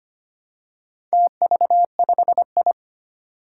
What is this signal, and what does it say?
Morse code, heard as a keyed tone, about 25 wpm, 700 Hz. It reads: TV5I
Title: Four Challenging Letter-Number Combinations 25wpm